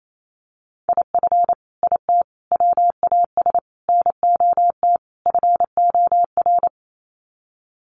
Morse code, heard as a keyed tone, about 28 wpm, 700 Hz. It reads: IF IT WAS NOT FOR
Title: Sets of 5 Words Encore 28wpm